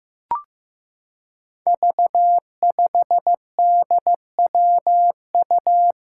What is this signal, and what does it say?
Morse code, heard as a keyed tone, about 15 wpm, 700 Hz. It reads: V5DWU
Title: Five Challenging Letter-Number Combinations 15wpm